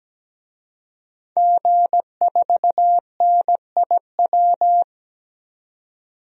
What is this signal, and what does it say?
Morse code, heard as a keyed tone, about 17 wpm, 700 Hz. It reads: G4NIW
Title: DXCC-100 Call Signs - 5 Characters - 17wpm